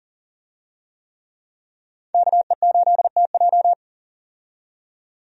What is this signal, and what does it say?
Morse code, heard as a keyed tone, about 40 wpm, 700 Hz. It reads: KE8TJ
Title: USA Call Signs - 5 Characters - 40wpm